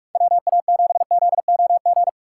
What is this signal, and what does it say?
Morse code, heard as a keyed tone, about 47 wpm, 700 Hz. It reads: WA7ZOG
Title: USA Call Signs - 3x Character Spacing - 45wpm